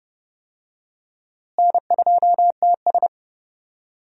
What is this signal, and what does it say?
Morse code, heard as a keyed tone, about 30 wpm, 700 Hz. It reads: N2TS
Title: USA Call Signs - 4 Characters - Repeated 3x - 30wpm